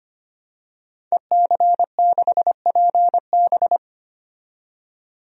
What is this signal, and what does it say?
Morse code, heard as a keyed tone, about 25 wpm, 700 Hz. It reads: EC6PB
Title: DXCC-100 Call Signs - 5 Characters - 25wpm